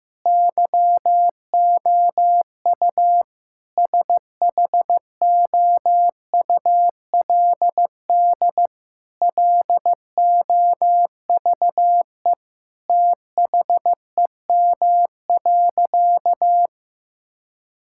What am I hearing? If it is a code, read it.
Morse code, heard as a keyed tone, about 15 wpm, 700 Hz. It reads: YOU SHOULD LOVE THEM.
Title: Sentences from Top 1600 Words 15wpm